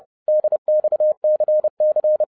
Morse code, heard as a keyed tone, about 30 wpm, 610 Hz. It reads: DXCC
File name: 2012-01-mp3 / n11-I DIDNT MISS THE SOLAR CYCLE 24-30wpm